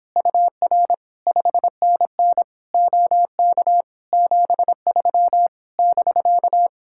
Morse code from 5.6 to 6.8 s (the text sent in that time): <BK>